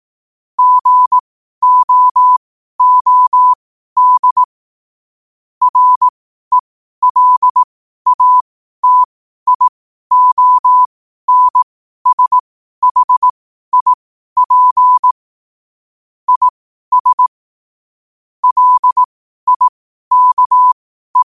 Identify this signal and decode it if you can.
Morse code, heard as a keyed tone, about 18 wpm, spaced as for 12 wpm, 1000 Hz. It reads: GOOD RELATIONSHIP IS LIKE
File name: qotdfiles / 12wpm